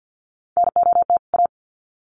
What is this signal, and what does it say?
Morse code, heard as a keyed tone, about 50 wpm, 700 Hz. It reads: NOT A
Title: Sets of 2 Words 50wpm